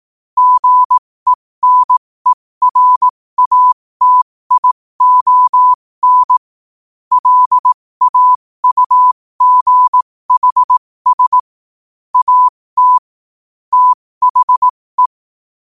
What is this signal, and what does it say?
Morse code, heard as a keyed tone, about 18 wpm, spaced as for 15 wpm, 1000 Hz. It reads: GENERATION LAUGHS AT THE
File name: qotdfiles / 15wpm